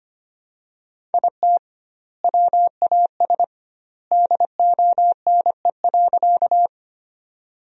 Morse code, heard as a keyed tone, about 25 wpm, 700 Hz. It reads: IT WAS DONE.
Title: 2x Word-Spacing - Sentences from Top 500 Words - 25wpm